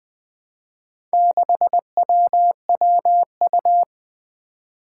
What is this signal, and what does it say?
Morse code, heard as a keyed tone, about 21 wpm, 700 Hz. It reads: 6WWU
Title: Four Challenging Letter-Number Combinations 20wpm